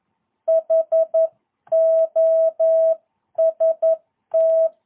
Morse code, heard as a keyed tone, about 11 wpm, 660 Hz. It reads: HOST
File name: jh8cbh / 2022cw